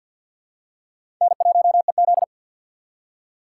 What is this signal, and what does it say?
Morse code, heard as a keyed tone, about 49 wpm, 700 Hz. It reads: N1EZ
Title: USA Call Signs - 4 Characters - 2x Character Spacing - 50wpm